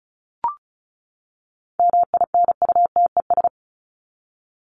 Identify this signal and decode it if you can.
Morse code, heard as a keyed tone, about 34 wpm, 700 Hz. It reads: MINUTES
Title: ES:S Suffix - Top 100 Words with Base Words 35wpm